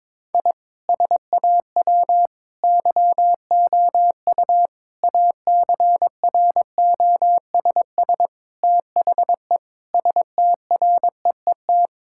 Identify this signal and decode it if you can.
Morse code, heard as a keyed tone, about 22 wpm, 700 Hz. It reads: I SAW YOU ACROSS THE STREET
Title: Sentences from Top 1000 Words - Review Part 5 - 22wpm